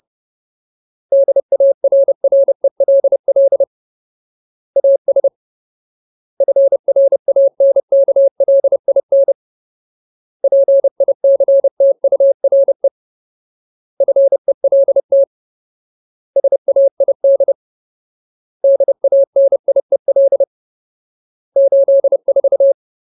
Morse code, heard as a keyed tone, about 30 wpm, 550 Hz. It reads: DARRELL AS FRANKLIN PICTURE FELT SAID DANIEL 84
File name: Ultimate Word List / 30WPM Ultimate